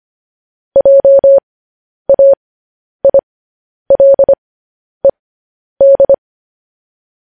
Morse code, heard as a keyed tone, about 25 wpm, spaced as for 10 wpm, 550 Hz. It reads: JAILED
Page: news headlines in morse code at 10 wpm